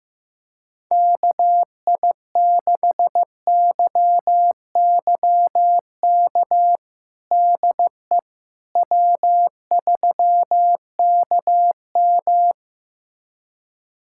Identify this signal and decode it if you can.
Morse code, heard as a keyed tone, about 15 wpm, 700 Hz. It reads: KI6YYK DE W3KM
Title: USA Callsign DE Callsign 15wpm